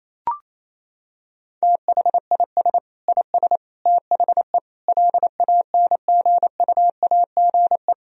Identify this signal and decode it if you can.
Morse code, heard as a keyed tone, about 28 wpm, 700 Hz. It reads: THIS IS THE LANGUAGE
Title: Sentences from Top 1000 Words - Review Part 3 - 28wpm